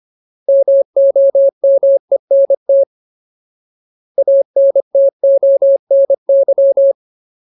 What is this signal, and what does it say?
Morse code, heard as a keyed tone, about 25 wpm, 550 Hz. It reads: MOMENT ANTONY
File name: Ultimate Word List / 25WPM Ultimate